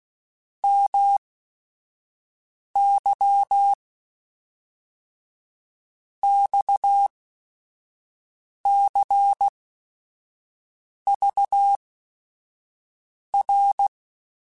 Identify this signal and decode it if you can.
Morse code, heard as a keyed tone, about 16 wpm, spaced as for 5 wpm, 800 Hz. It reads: MY XCVR
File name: QSOs / QSOs-mp3s / WD0FBT 1@5